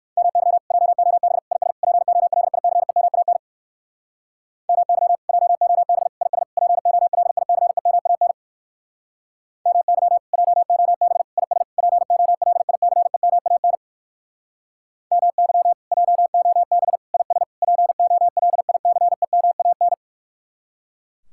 Morse code, heard as a keyed tone, about 56 wpm, 700 Hz. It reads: MY JOB IS POLICEMAN MY JOB IS POLICEMAN MY JOB IS POLICEMAN MY JOB IS POLICEMAN
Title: Speed-Racing Common QSO Phrases Encore - 45wpm